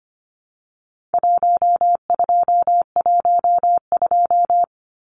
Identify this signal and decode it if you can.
Morse code, heard as a keyed tone, about 25 wpm, 700 Hz. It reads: 1212
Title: Numbers - Contest Style - 4-Digit 25wpm